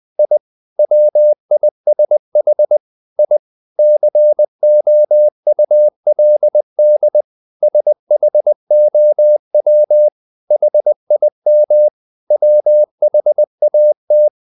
Morse code, heard as a keyed tone, about 20 wpm, 600 Hz. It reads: I WISH I COULD SHOW HIM WHAT
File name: the old man and the sea / the old man and the sea part 1 20wpm